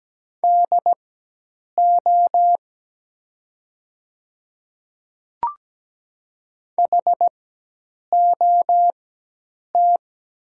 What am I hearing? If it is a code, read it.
Morse code, heard as a keyed tone, about 17 wpm, spaced as for 8 wpm, 700 Hz. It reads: DO HOT
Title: 4x Character Spacing - Top 100 Words - 17wpm